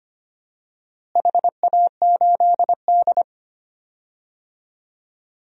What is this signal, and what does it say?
Morse code, heard as a keyed tone, about 25 wpm, 700 Hz. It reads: HA8D
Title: DXCC-100 Call Signs - 4 Characters - Repeated 3x - 25wpm